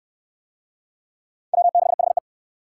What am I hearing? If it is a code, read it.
Morse code, heard as a keyed tone, about 58 wpm, 700 Hz. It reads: W6HE